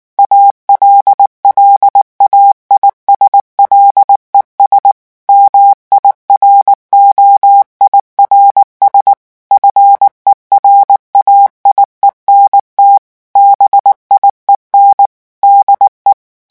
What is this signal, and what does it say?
Morse code, heard as a keyed tone, about 19 wpm, 800 Hz. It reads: ALLAIS LES MIROIRS FERAIENT BIEN DE
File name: morse mp3 / revision9 citations humoristiques vitesse 18 mots minute